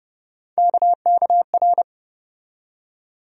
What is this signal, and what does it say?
Morse code, heard as a keyed tone, about 30 wpm, 700 Hz. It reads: KKR
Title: USA Call Sign Postfixes - 3 Letters 30wpm